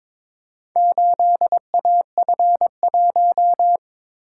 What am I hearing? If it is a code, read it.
Morse code, heard as a keyed tone, about 22 wpm, 700 Hz. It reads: 8AF1
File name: practice / 22wpm / Four Letter-Number Combinations - 22wpm